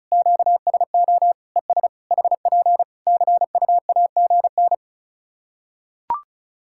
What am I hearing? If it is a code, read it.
Morse code, heard as a keyed tone, about 36 wpm, 700 Hz. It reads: QSO ES HP CUAGN
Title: Practice QSO - Exchange 4 - Ending - 35wpm